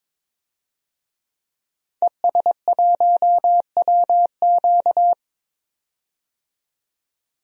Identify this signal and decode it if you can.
Morse code, heard as a keyed tone, about 22 wpm, 700 Hz. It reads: ES1WQ
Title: DXCC-100 Call Signs - 5 Characters - 22wpm